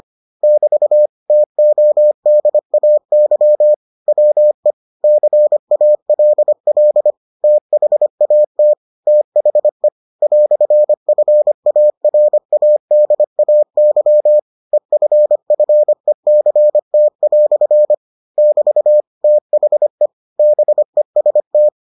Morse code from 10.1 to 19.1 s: "FARADAY EFFECT" <BT>

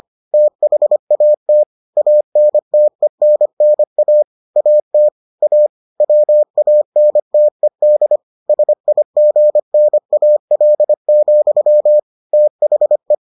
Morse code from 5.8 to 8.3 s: WANTED

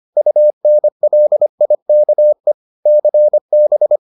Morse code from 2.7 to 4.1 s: CB

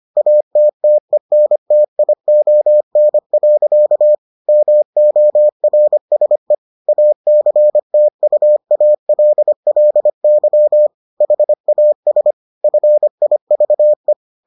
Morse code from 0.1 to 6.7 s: ATTENTION. MORSE

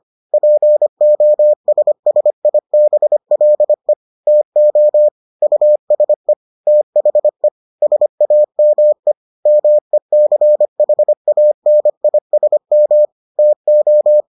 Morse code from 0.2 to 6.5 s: POSSIBLE TO USE

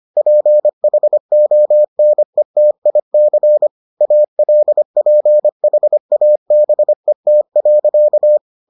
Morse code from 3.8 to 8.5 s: ALPHABET.

